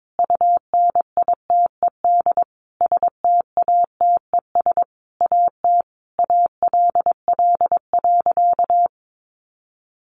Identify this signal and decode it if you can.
Morse code, heard as a keyed tone, about 22 wpm, 700 Hz. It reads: UNITED STATES AT ALL.
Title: A Sound Of Thunder 22wpm